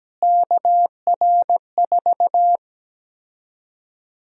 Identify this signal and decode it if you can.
Morse code, heard as a keyed tone, about 17 wpm, 700 Hz. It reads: KR4
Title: USA Call Sign Prefixes - Repeated 2x - 17wpm